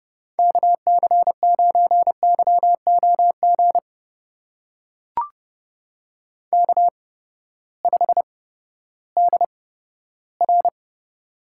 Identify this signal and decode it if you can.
Morse code, heard as a keyed tone, about 30 wpm, spaced as for 15 wpm, 700 Hz. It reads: KC9YOG K5DR
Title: USA Call Signs - 8x Character Spacing - 30wpm